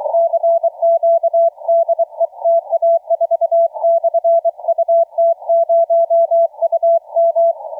Morse code, heard as a keyed tone, about 24 wpm, 670 Hz, with radio noise. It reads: CQ DE TA4/UT0UM
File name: TA4 UT0UM 1525UTC 20m